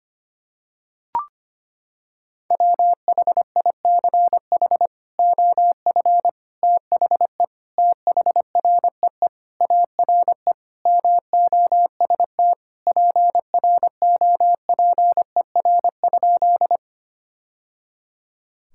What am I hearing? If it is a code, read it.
Morse code, heard as a keyed tone, about 25 wpm, 700 Hz. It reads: WHICH OF THE THREE ARE MOST PROPER?
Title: Sentences from Top 1000 Words - Review Part 5 - 25wpm